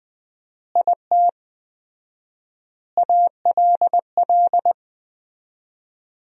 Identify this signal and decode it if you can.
Morse code, heard as a keyed tone, about 20 wpm, 700 Hz. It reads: IT ALL
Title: 4x Word-Spacing - Sentences from Top 200 Words - 20wpm